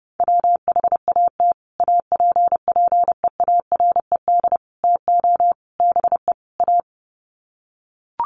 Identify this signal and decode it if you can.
Morse code, heard as a keyed tone, about 30 wpm, 700 Hz. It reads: WHAT APPEARED TO BE A
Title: Sets of 5 Words Encore2 30wpm